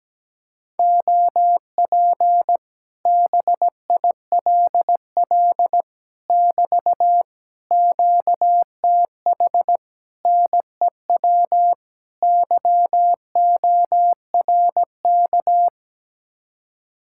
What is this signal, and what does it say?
Morse code, heard as a keyed tone, about 17 wpm, 700 Hz. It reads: OP BILL <BT> QTH NEW YORK
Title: Speed-Racing Common QSO Phrases - 17wpm